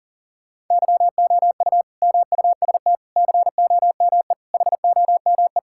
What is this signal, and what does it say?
Morse code, heard as a keyed tone, about 37 wpm, 700 Hz. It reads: YOU MUST COME HOME
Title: Sentences from Top 1000 Words - Review Part 5 - 40wpm